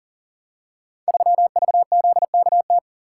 Morse code, heard as a keyed tone, about 40 wpm, 700 Hz. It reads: 3VZKT
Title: Code Groups - 5 Letter:Numbers - 40wpm